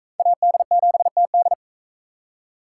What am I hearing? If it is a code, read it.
Morse code, heard as a keyed tone, about 42 wpm, 700 Hz. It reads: AD7TD